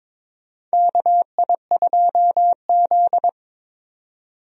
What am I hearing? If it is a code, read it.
Morse code, heard as a keyed tone, about 22 wpm, 700 Hz. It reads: KI2Z
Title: USA Call Signs - 4 Characters - 22wpm